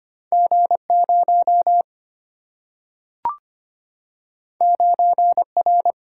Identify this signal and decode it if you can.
Morse code, heard as a keyed tone, about 25 wpm, 700 Hz. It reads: G0 9R
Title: Single-Letter Single-Number 25wpm